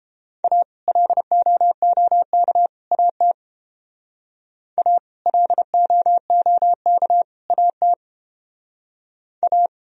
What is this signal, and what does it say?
Morse code, heard as a keyed tone, about 30 wpm, 700 Hz. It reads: A LOOK AT A LOOK AT A